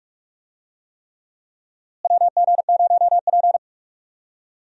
Morse code, heard as a keyed tone, about 46 wpm, 700 Hz. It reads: WG0P
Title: USA Call Signs - 4 Characters - Repeated 3x - 45wpm